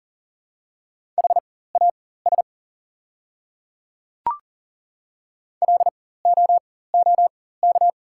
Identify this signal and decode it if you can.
Morse code, heard as a keyed tone, about 40 wpm, spaced as for 19 wpm, 690 Hz. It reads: HAS LOOK
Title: 4x Character Spacing - Top 100 Words - 40wpm